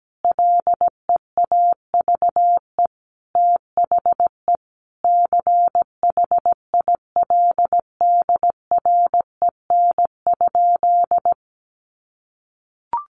Morse code, heard as a keyed tone, about 17 wpm, 700 Hz. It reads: LEAVE THE CHILDREN?
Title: Sentences from Top 1400 Words 17wpm